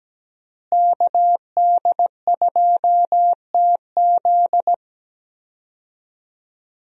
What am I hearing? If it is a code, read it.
Morse code, heard as a keyed tone, about 17 wpm, 700 Hz. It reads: KD2TZ
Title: USA Call Signs - 5 Characters - Repeated 3x - 17wpm